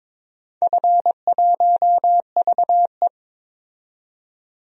Morse code, heard as a keyed tone, about 22 wpm, 700 Hz. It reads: F1VE
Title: DXCC-100 Call Signs - 4 Characters - 22wpm